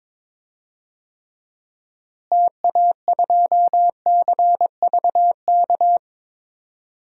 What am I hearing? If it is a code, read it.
Morse code, heard as a keyed tone, about 23 wpm, 700 Hz. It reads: TA2CVK
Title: DXCC-100 Call Signs - 6 Characters - Repeated 3x - 22wpm